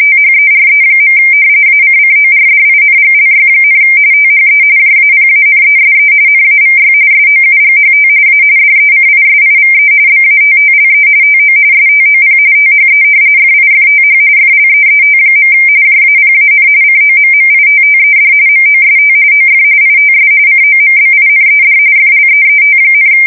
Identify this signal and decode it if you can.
RTTY, about 45.45 baud. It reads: THERE. ACCORDINGLY, THE JAPANESE NAVY SOUGHT A DECISIVE BATTLE TO STOP THE FIFTH FLEET. JAPAN COMMITTED MOST OF ITS COMBINED FLEET,
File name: fsk170 / battleshipnj